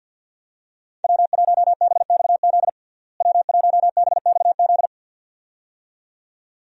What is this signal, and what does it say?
Morse code, heard as a keyed tone, about 47 wpm, 700 Hz. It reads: W1BXZ W1BXZ
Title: USA Call Signs - 5 Characters - Repeated 2x - 50wpm